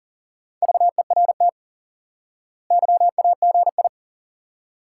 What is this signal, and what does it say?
Morse code, heard as a keyed tone, about 40 wpm, 700 Hz. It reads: VERT YAGI
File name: IQE Course 3 - Lesson 89 - YAGI - Two Elements - 40wpm